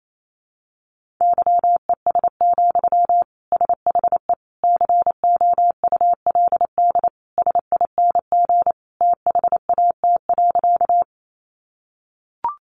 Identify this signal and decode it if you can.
Morse code, heard as a keyed tone, about 28 wpm, 700 Hz. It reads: YES, SHE COULD SING THAT.
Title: Sentences from Top 500 Words 28wpm